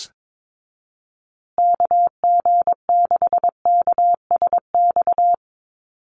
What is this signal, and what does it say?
Morse code, heard as a keyed tone, about 22 wpm, 690 Hz. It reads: KG6KSX
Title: USA Call Signs Repeated 3x 22wpm